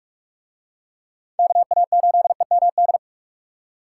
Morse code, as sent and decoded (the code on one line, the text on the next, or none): -.- .- ---.. -- -..
KA8EMD